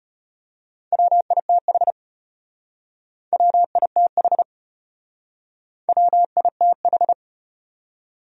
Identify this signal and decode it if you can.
Morse code, heard as a keyed tone, about 35 wpm, 700 Hz. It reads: WITH WITH WITH